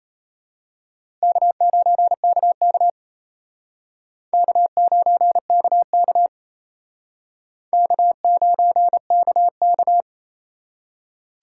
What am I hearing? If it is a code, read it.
Morse code, heard as a keyed tone, about 31 wpm, 700 Hz. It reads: K9KK K9KK K9KK
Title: Speed Racing - USA 1x2 Call Signs 28wpm